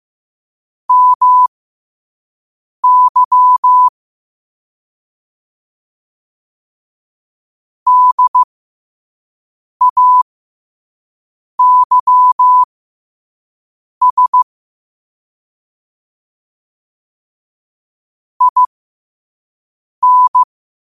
Morse code, heard as a keyed tone, about 15 wpm, spaced as for 5 wpm, 1000 Hz. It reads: MY DAYS IN
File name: qotdfiles / 5wpm